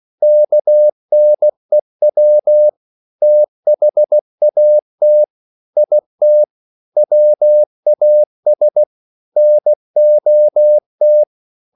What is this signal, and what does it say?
Morse code, heard as a keyed tone, about 16 wpm, 600 Hz. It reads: KNEW THAT IT WAS NOT